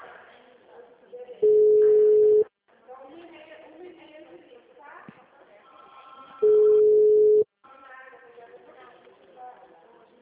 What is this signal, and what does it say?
Ringback tone (Europe, 425 Hz), with speech.
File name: out- -2008- - - .11371